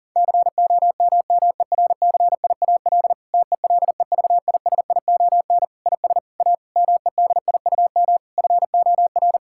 Morse code, heard as a keyed tone, about 37 wpm, 700 Hz. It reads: COMMERCIAL TELEVISION IS A MEDIUM FOR
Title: Sentences from Top 1000 Words - Review Part 6 - 40wpm